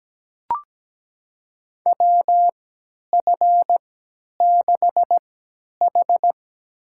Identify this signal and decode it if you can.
Morse code, heard as a keyed tone, about 17 wpm, spaced as for 10 wpm, 700 Hz. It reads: WF6H
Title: USA Call Signs - 4 Characters - 3x Character Spacing - 17wpm